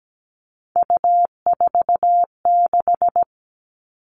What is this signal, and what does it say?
Morse code, heard as a keyed tone, about 17 wpm, 700 Hz. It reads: U46
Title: All Cut Numbers - 3-Digit 17wpm